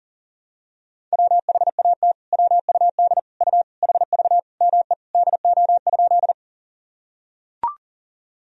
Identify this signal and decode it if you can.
Morse code, heard as a keyed tone, about 37 wpm, 700 Hz. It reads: WHAT WUD U HV ME DO?
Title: Sentences from Top 200 Words and Abbreviations 40wpm